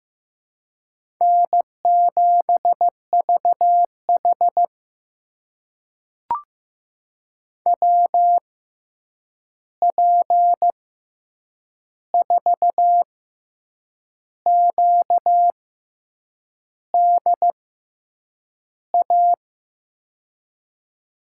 Morse code, heard as a keyed tone, about 15 wpm, spaced as for 7 wpm, 700 Hz. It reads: N7VH WP4QDA
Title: USA Call Signs - 6x Character Spacing - 15wpm